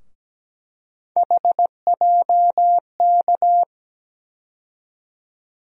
Morse code, heard as a keyed tone, about 17 wpm, 700 Hz. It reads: HJK
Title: USA Call Sign Postfixes - 3 Letters 17wpm